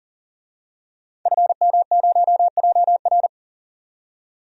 Morse code, heard as a keyed tone, about 40 wpm, 700 Hz. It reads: FM0JR
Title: DXCC-100 Call Signs - 5 Characters - 40wpm